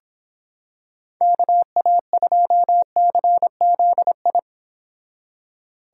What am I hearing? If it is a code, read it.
Morse code, heard as a keyed tone, about 26 wpm, 700 Hz. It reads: KA2CZI